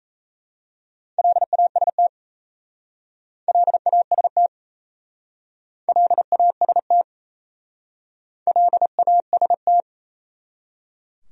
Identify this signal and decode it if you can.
Morse code, heard as a keyed tone, about 36 wpm, 700 Hz. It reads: LAST LAST LAST LAST